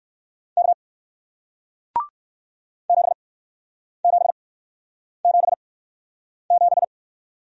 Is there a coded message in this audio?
Yes: Morse code from 0.6 to 6.8 s, heard as a keyed tone, about 60 wpm, spaced as for 9 wpm, 700 Hz. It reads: K 7777